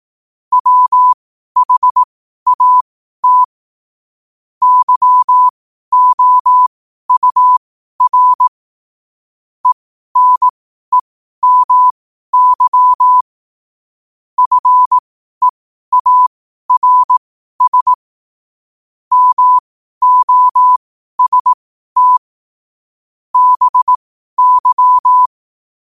Morse code, heard as a keyed tone, about 18 wpm, spaced as for 12 wpm, 1000 Hz. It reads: WHAT YOUR ENEMY FEARS MOST BY
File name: qotdfiles / 12wpm